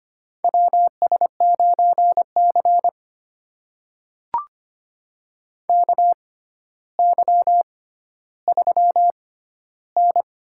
Morse code, heard as a keyed tone, about 25 wpm, spaced as for 13 wpm, 700 Hz. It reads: WS9C KY3N
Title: USA Call Signs - 4 Characters - 6x Character Spacing - 25wpm